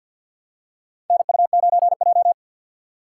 Morse code, heard as a keyed tone, about 48 wpm, 700 Hz. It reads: NU9J
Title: USA Call Signs - 4 Characters - 2x Character Spacing - 50wpm